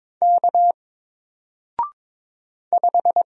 Morse code, heard as a keyed tone, about 22 wpm, 700 Hz. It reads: K 5
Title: Single Letter-Number 22wpm